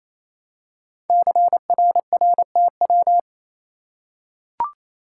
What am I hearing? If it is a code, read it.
Morse code, heard as a keyed tone, about 27 wpm, 700 Hz. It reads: CRRTW
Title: Code Groups - 5 Letters - 28wpm